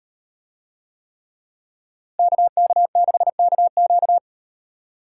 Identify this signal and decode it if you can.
Morse code, heard as a keyed tone, about 38 wpm, 700 Hz. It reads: KK6KQ